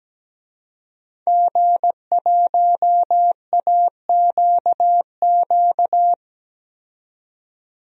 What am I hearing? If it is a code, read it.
Morse code, heard as a keyed tone, about 17 wpm, 700 Hz. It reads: G1AQQ
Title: DXCC-100 Call Signs - 5 Characters - 17wpm